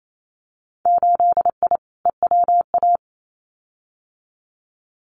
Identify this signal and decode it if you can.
Morse code, heard as a keyed tone, about 28 wpm, 700 Hz. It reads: 8I EWA
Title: Winter Field Day Exchanges Encore 28wpm